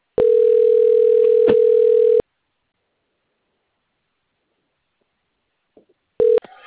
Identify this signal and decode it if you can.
Ringback tone (North America), 440 + 480 Hz.